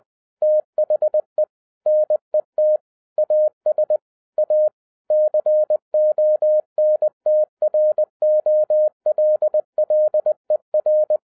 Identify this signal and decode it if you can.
Morse code, heard as a keyed tone, about 20 wpm, 610 Hz. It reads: THE NET AS A CONTROLLER
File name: 2012-01-mp3 / n16-THROUGH EIGHT MONTHS-20wpm